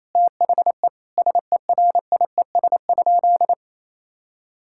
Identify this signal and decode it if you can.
Morse code, heard as a keyed tone, about 28 wpm, 700 Hz. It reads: THE SERIES?
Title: Sentences from Top 1000 Words - Review Part 5 - 28wpm